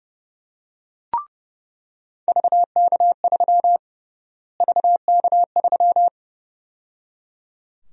Morse code, heard as a keyed tone, about 32 wpm, 700 Hz. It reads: VK3 VK3
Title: USA Call Sign Prefixes - Repeated 2x - 30wpm